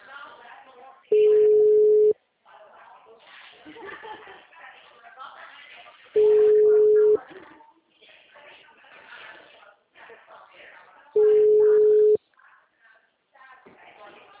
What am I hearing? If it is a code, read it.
Ringback tone (Europe, 425 Hz), with speech.